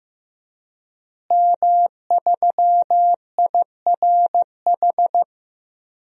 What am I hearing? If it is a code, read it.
Morse code, heard as a keyed tone, about 15 wpm, 700 Hz. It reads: M3IRH
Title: DXCC-100 Call Signs - 5 Characters - Repeated 2x - 15wpm